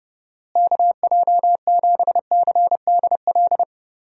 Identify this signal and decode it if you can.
Morse code, heard as a keyed tone, about 30 wpm, 700 Hz. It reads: KJ7CDL
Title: USA Call Signs - 6 Characters - Repeated 3x - 30wpm